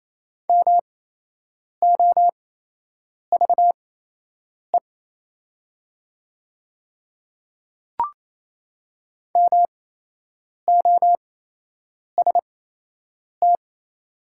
Morse code, heard as a keyed tone, about 28 wpm, spaced as for 8 wpm, 700 Hz. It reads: MOVE MOST
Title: 8x Character Spacing - Top 200 Words - 28wpm